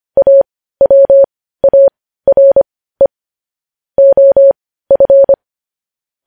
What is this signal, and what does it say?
Morse code, heard as a keyed tone, about 25 wpm, spaced as for 15 wpm, 550 Hz. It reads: AWARE OF
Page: News headlines in morse code at 15 wpm